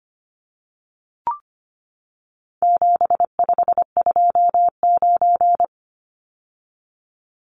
Morse code, heard as a keyed tone, about 25 wpm, 700 Hz. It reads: 7529
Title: Numbers - 4-Digit 25wpm